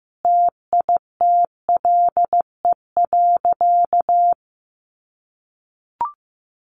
Morse code, heard as a keyed tone, about 14 wpm, 700 Hz. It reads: TITLE.
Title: Sentences from Top 1200 Words 15wpm